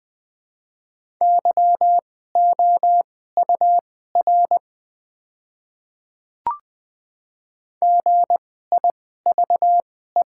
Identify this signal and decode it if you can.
Morse code, heard as a keyed tone, about 20 wpm, spaced as for 14 wpm, 700 Hz. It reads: YOUR GIVE
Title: 2x Character Spacing - Top 500 Words - 20wpm